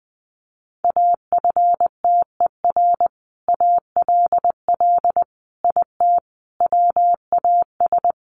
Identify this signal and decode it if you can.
Morse code, heard as a keyed tone, about 20 wpm, 700 Hz. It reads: AFTER ALL IT WAS
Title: Sentences from Top 1200 Words 20wpm